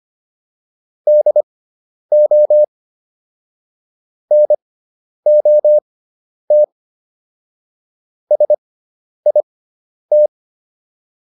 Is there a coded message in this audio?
Yes: Morse code from 1.1 to 10.3 s, heard as a keyed tone, about 25 wpm, spaced as for 10 wpm, 600 Hz. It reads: DO NOT SIT